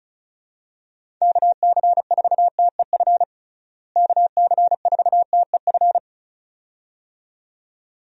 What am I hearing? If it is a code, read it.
Morse code, heard as a keyed tone, about 36 wpm, 700 Hz. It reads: KC4TEF KC4TEF
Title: USA Call Signs - 6 Characters - Repeated 2x - 35wpm